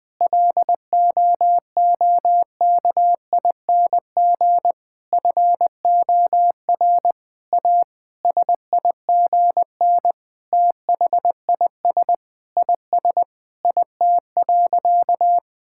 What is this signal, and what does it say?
Morse code, heard as a keyed tone, about 20 wpm, 700 Hz. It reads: LOOKING FOR A SIGN THIS IS IT.
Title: Sentences with 6-Word Phrases - 20wpm